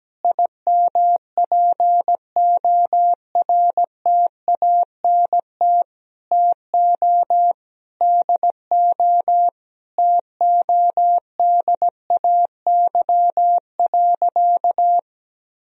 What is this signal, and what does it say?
Morse code, heard as a keyed tone, about 17 wpm, 700 Hz. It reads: IMPORTANT TO DO TODAY.
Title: Sentences from Top 1000 Words - Review Part 4 - 17wpm